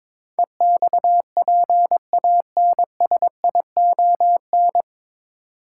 Morse code, heard as a keyed tone, about 21 wpm, 700 Hz. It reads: EXPANSION
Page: Morse Code Ninja practice 22wpm